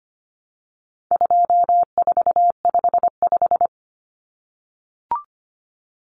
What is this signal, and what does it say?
Morse code, heard as a keyed tone, about 25 wpm, 700 Hz. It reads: 2455
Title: Common Cut Numbers - 4-Digit 25wpm